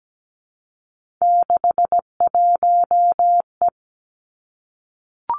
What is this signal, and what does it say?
Morse code, heard as a keyed tone, about 17 wpm, 700 Hz. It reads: 61E
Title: Three Challenging Letter-Number Combinations 17wpm